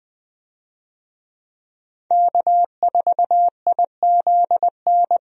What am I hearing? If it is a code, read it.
Morse code, heard as a keyed tone, about 20 wpm, 700 Hz. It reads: K4IZN
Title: USA Call Signs - 5 Characters - Repeated 2x - 20wpm